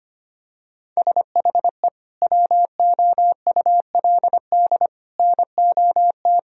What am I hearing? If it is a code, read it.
Morse code, heard as a keyed tone, about 25 wpm, 700 Hz. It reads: SHE WOULD NOT
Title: Sentences from Top 1000 Words - Review Part 3 - 25wpm